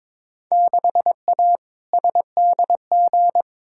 Morse code, heard as a keyed tone, about 22 wpm, 700 Hz. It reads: 6A SDG
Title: Field Day Exchanges Encore - 22wpm